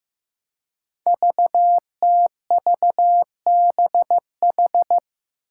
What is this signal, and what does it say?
Morse code, heard as a keyed tone, about 15 wpm, 700 Hz. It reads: VTVBH
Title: Five Challenging Letter-Number Combinations 15wpm